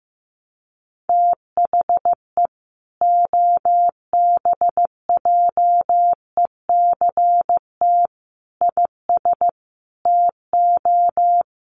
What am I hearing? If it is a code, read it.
Morse code, heard as a keyed tone, about 15 wpm, 700 Hz. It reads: THE OBJECT IS TO
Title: Sentences from Top 1300 Words 15wpm